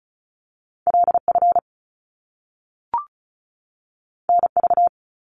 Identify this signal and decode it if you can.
Morse code, heard as a keyed tone, about 36 wpm, 700 Hz. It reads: LF NV
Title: Top Two Letter Combinations in Words 35wpm